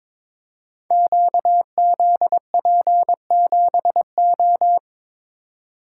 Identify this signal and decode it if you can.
Morse code, heard as a keyed tone, about 22 wpm, 700 Hz. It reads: QZP7O
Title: Code Groups - 5 Letter:Numbers - 22wpm